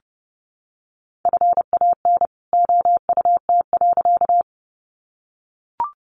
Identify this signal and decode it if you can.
Morse code, heard as a keyed tone, about 30 wpm, 700 Hz. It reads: FAN OUT.
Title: Sentences from Top 1300 Words 30wpm